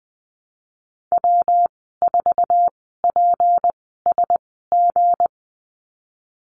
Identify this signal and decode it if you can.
Morse code, heard as a keyed tone, about 20 wpm, spaced as for 14 wpm, 700 Hz. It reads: W4PSG
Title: 193 Lesson 31g - Letter v - Call Signs 20wpm